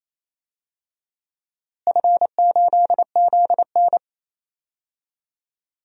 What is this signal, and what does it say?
Morse code, heard as a keyed tone, about 28 wpm, 700 Hz. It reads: F8ZN
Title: DXCC-100 Call Signs - 4 Characters - Repeated 3x - 28wpm